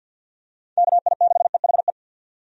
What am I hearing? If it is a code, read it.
Morse code, heard as a keyed tone, about 52 wpm, 700 Hz. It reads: KI6EHE
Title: USA Call Signs - 6 Characters - 50wpm